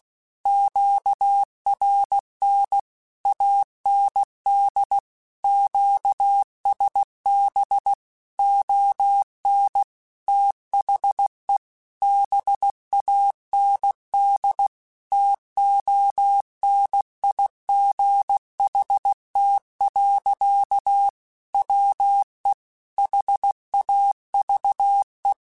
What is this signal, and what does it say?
Morse code, heard as a keyed tone, about 16 wpm, 800 Hz. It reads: QRN AND QSB ON THE BAND TONIGHT. WE HAVE
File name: QSOs / QSOs-mp3s / WD0BGP 1@15